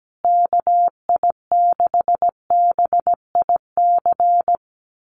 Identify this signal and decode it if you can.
Morse code, heard as a keyed tone, about 17 wpm, 700 Hz. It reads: KI6BIC
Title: USA Call Signs Repeated 3x 17wpm